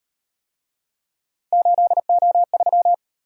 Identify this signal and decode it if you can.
Morse code, heard as a keyed tone, about 38 wpm, 700 Hz. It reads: 8O3